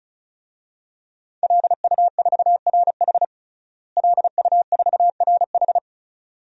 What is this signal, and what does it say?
Morse code, heard as a keyed tone, about 36 wpm, 700 Hz. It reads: LU4RH LU4RH
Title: DXCC-100 Call Signs - 5 Characters - Repeated 2x - 35wpm